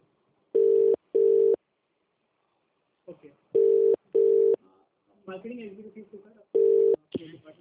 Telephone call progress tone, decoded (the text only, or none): ringback tone